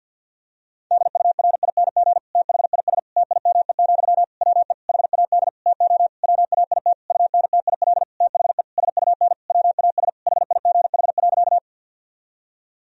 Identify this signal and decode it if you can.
Morse code, heard as a keyed tone, about 50 wpm, 700 Hz. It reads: DURING THIS TIME, WE HAD TO WAIT UNTIL THE SUN WAS HIGH.
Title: Sentences from Top 1000 Words - Review Part 2 - 50wpm